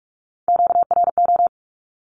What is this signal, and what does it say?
Morse code, heard as a keyed tone, about 44 wpm, 700 Hz. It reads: QRO
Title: Q-Codes 45wpm